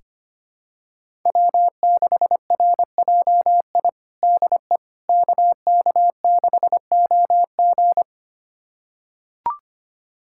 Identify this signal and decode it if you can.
Morse code, heard as a keyed tone, about 25 wpm, 700 Hz. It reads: W6RJI DE KK6OG
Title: USA Callsign DE Callsign - 5 Characters - 25wpm